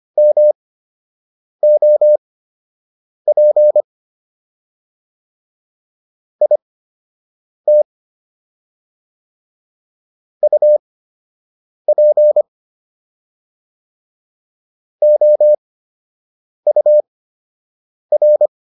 Morse code, heard as a keyed tone, about 25 wpm, spaced as for 7 wpm, 600 Hz. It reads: MOP IT UP OUR